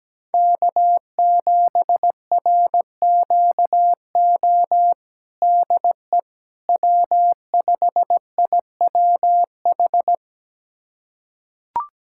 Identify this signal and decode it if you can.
Morse code, heard as a keyed tone, about 17 wpm, 700 Hz. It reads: K7RQO DE W5IWH
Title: USA Callsign DE Callsign - 5 Characters - 17wpm